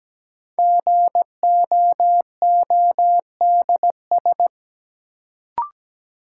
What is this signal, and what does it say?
Morse code, heard as a keyed tone, about 17 wpm, 700 Hz. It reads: GOODS
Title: Sentences from Top 1000 Words - Review Part 6 - New Word Drill - 17wpm